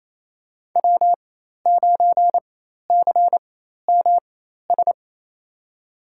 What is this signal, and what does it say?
Morse code, heard as a keyed tone, about 28 wpm, spaced as for 13 wpm, 700 Hz. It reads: W9CMS
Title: USA Call Signs - 5 Characters - 4x Character Spacing - 28wpm